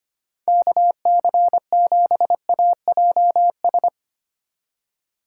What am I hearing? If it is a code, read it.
Morse code, heard as a keyed tone, about 25 wpm, 700 Hz. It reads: KC7AJS
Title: USA Call Signs - 6 Characters - 2x Character Spacing - 25wpm